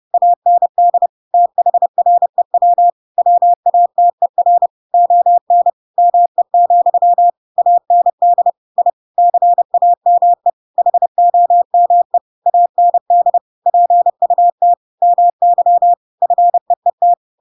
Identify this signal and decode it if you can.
Morse code, heard as a keyed tone, about 30 wpm, 700 Hz. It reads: AND THREW WATER ON ME, AND I CAME HOME AND PUT MY FEET
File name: Pinocchio / Pinocchio ch7 30 30 ik1hge